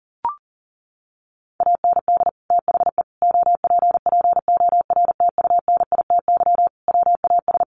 Morse code, heard as a keyed tone, about 40 wpm, 700 Hz. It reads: AND THE OPPORTUNITY WAS
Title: Sentences from Top 1400 Words 40wpm